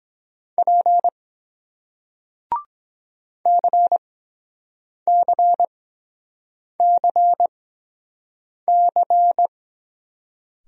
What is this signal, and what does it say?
Morse code, heard as a keyed tone, about 24 wpm, spaced as for 7 wpm, 700 Hz. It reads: P CCCC